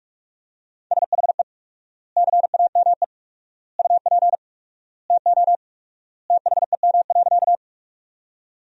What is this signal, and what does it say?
Morse code, heard as a keyed tone, about 42 wpm, 700 Hz. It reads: SHE CAME UP TO THEM.
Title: 4x Word-Spacing - Sentences from Top 200 Words - 45wpm